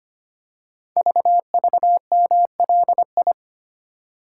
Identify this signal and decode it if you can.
Morse code, heard as a keyed tone, about 25 wpm, 700 Hz. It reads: VVMLI
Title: Code Groups - 5 Letters - 25wpm